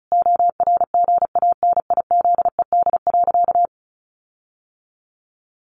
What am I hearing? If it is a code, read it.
Morse code, heard as a keyed tone, about 34 wpm, 700 Hz. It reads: ORGANIZED.
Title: Sentences from Top 1300 Words 35wpm